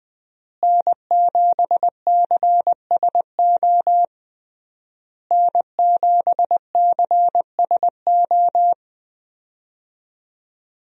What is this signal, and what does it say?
Morse code, heard as a keyed tone, about 21 wpm, 700 Hz. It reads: N7CSO N7CSO
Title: USA Call Signs - 5 Characters - Repeated 3x - 20wpm